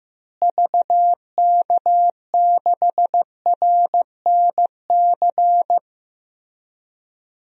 Morse code, heard as a keyed tone, about 15 wpm, 700 Hz. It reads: VK6RNC
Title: DXCC-100 Call Signs - 6 Characters - Repeated 3x - 15wpm